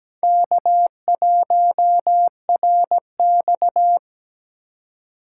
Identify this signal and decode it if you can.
Morse code, heard as a keyed tone, about 20 wpm, 700 Hz. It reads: K1RX
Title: Speed Racing - USA 1x2 Call Signs 17wpm